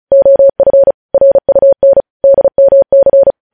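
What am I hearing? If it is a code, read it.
Morse code, heard as a keyed tone, about 36 wpm, 550 Hz. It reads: OF RUN DMC
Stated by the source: Morse code news headlines at 35.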